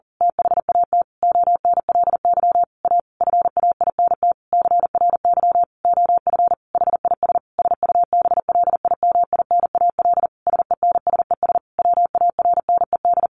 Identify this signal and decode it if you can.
Morse code, heard as a keyed tone, about 40 wpm, 700 Hz. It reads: THAT ONLY A FAINT CRY OF HIS SUBLIMINAL SENSES WARNED
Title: A Sound Of Thunder 40wpm